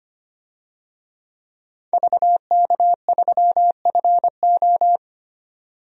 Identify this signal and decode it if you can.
Morse code, heard as a keyed tone, about 25 wpm, 700 Hz. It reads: VK3FO
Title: DXCC-100 Call Signs - 5 Characters - 25wpm